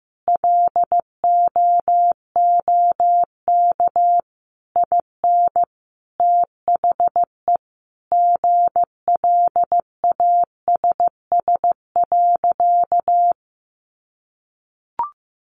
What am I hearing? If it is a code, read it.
Morse code, heard as a keyed tone, about 15 wpm, 700 Hz. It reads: LOOK IN THE GLASS.
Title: Sentences from Top 1300 Words 15wpm